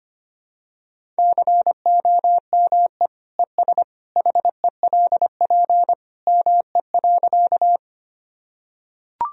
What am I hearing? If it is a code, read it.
Morse code, heard as a keyed tone, about 25 wpm, 700 Hz. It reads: COME ES HELP ME.
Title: Sentences from Top 200 Words and Abbreviations 25wpm